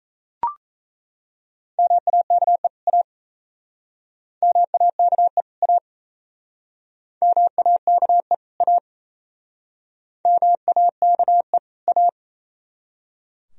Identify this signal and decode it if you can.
Morse code, heard as a keyed tone, about 36 wpm, 700 Hz. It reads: MAKE A MAKE A MAKE A MAKE A